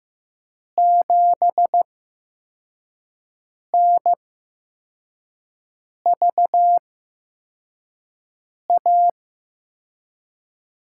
Morse code, heard as a keyed tone, about 15 wpm, spaced as for 4 wpm, 700 Hz. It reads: 7NVA
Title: USA Call Signs - 5 Characters - 8x Character Spacing - 15wpm